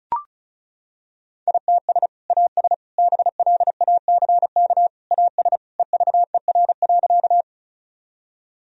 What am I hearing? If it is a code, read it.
Morse code, heard as a keyed tone, about 36 wpm, 700 Hz. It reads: ITS AS BLACK AS EVER.
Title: Sentences from Top 1000 Words - Review Part 2 - 35wpm